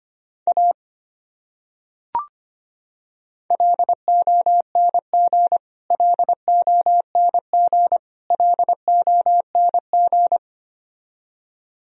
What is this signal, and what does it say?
Morse code, heard as a keyed tone, about 25 wpm, 700 Hz. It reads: A LONG LONG LONG
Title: Top 100 Words Repeated 25wpm